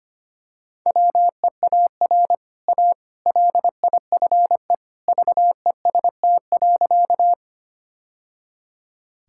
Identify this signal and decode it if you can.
Morse code, heard as a keyed tone, about 25 wpm, 700 Hz. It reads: WEAR A LIFE VEST.
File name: Easy Sentences with Hints - 25wpm